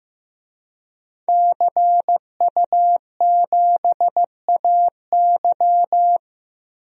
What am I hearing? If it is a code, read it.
Morse code, heard as a keyed tone, about 16 wpm, 700 Hz. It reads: CU7AY
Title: DXCC-100 Call Signs - 5 Characters - Repeated 3x - 15wpm